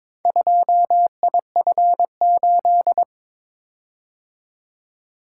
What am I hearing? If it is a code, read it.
Morse code, heard as a keyed tone, about 22 wpm, 700 Hz. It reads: 2IF8
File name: practice / 22wpm / Four Letter-Number Combinations - 22wpm